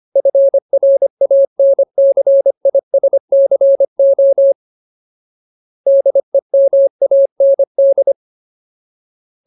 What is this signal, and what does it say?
Morse code, heard as a keyed tone, about 25 wpm, 550 Hz. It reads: FRANCISCO DEMAND
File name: Ultimate Word List / 25WPM Ultimate